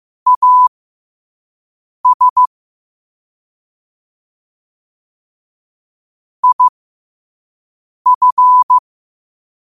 Morse code, heard as a keyed tone, about 15 wpm, spaced as for 5 wpm, 1000 Hz. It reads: AS IF
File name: qotdfiles / 5wpm